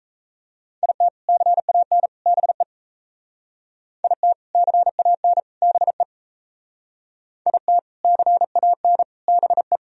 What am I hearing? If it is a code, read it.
Morse code, heard as a keyed tone, about 35 wpm, 700 Hz. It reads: IT CAN BE IT CAN BE IT CAN BE
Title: Speed-Racing - Sets of 3 Words 28wpm